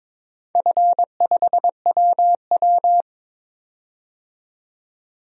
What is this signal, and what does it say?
Morse code, heard as a keyed tone, about 21 wpm, 700 Hz. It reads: F5WW
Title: DXCC-100 Call Signs - 4 Characters - 22wpm